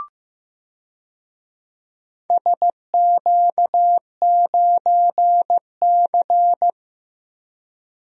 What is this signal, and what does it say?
Morse code, heard as a keyed tone, about 15 wpm, 700 Hz. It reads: SQ9C
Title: DXCC-100 Call Signs - 4 Characters - 15wpm